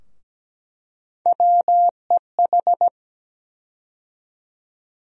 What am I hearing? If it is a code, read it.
Morse code, heard as a keyed tone, about 17 wpm, 700 Hz. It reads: WEH
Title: USA Call Sign Postfixes - 3 Letters 17wpm